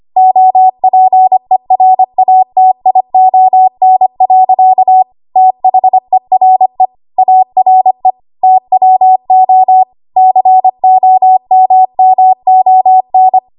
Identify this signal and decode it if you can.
Morse code, heard as a keyed tone, about 25 wpm, 750 Hz. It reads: OPERATION. THERE ARE TWO COMMON
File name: Archive / 25 WPM / 25WPM